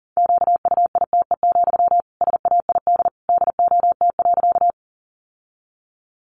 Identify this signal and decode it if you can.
Morse code, heard as a keyed tone, about 40 wpm, 700 Hz. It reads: QUITE, SAID DOT.
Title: Sentences from Top 1500 Words 40wpm